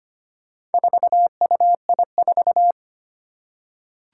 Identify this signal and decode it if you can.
Morse code, heard as a keyed tone, about 25 wpm, 700 Hz. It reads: 4UI4
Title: Four Challenging Letter-Number Combinations 25wpm